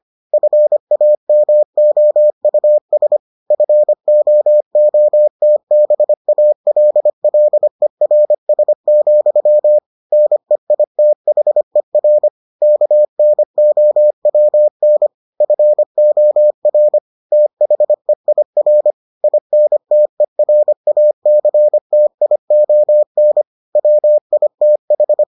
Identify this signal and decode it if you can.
Morse code, heard as a keyed tone, about 25 wpm, 600 Hz. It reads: FAMOUS FOOTBALLERS, NEITHER KNOWN FOR THEIR INTERACTION WITH